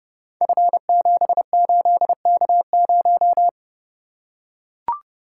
Morse code, heard as a keyed tone, about 30 wpm, 700 Hz. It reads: F78K0
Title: Code Groups - 5 Letter:Numbers - 30wpm